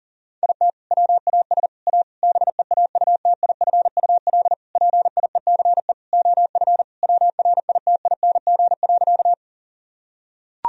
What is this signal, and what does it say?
Morse code, heard as a keyed tone, about 40 wpm, 700 Hz. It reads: IT WAS A BEAUTIFUL PIECE OF WRITING.
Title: Sentences from Top 1000 Words - Review Part 2 - 40wpm